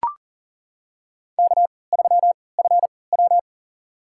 Morse code, heard as a keyed tone, about 40 wpm, spaced as for 23 wpm, 700 Hz. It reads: K3FW
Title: USA Call Signs - 4 Characters - 3x Character Spacing - 40wpm